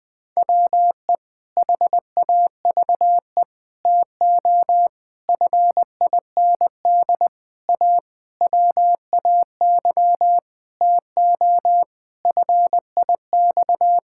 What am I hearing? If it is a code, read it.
Morse code, heard as a keyed tone, about 20 wpm, 700 Hz. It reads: WE HAVE TO FIND A WAY TO FIX
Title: Sentences with 7-Word Phrases - Spoken Beginning - 20wpm